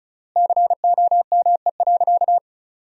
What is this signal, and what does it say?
Morse code, heard as a keyed tone, about 36 wpm, 700 Hz. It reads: COME.
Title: 3x Word-Spacing - Sentences from Top 100 Words - 35wpm